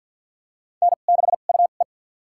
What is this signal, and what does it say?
Morse code, heard as a keyed tone, about 46 wpm, spaced as for 35 wpm, 700 Hz. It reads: N6UE
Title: USA Call Signs - 4 Characters - 2x Character Spacing - 50wpm